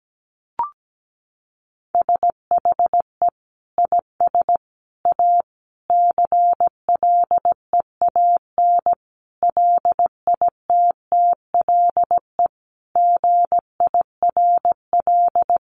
Morse code, heard as a keyed tone, about 17 wpm, 700 Hz. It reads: SHE IS A CLEAN LITTLE GIRL
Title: Sentences from Top 1100 Words 17wpm